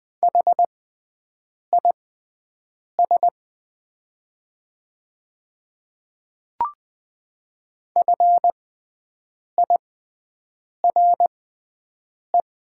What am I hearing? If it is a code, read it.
Morse code, heard as a keyed tone, about 20 wpm, spaced as for 7 wpm, 700 Hz. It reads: HIS FIRE
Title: 6x Character Spacing - Top 500 Words - 20wpm